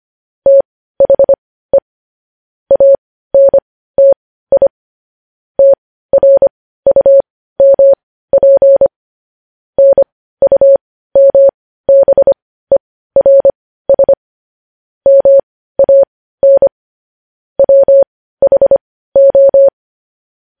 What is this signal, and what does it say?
Morse code, heard as a keyed tone, about 25 wpm, spaced as for 15 wpm, 550 Hz. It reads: THE ANTI TRUMP NUMBERS MAN WHO
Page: news headlines in morse code at 15 wpm